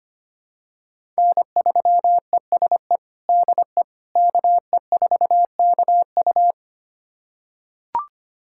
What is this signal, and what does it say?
Morse code, heard as a keyed tone, about 25 wpm, 700 Hz. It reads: N3ESE DE KE4KU
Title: USA Callsign DE Callsign - 5 Characters - 25wpm